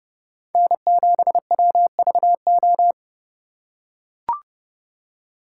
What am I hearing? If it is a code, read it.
Morse code, heard as a keyed tone, about 30 wpm, 700 Hz. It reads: N7WVO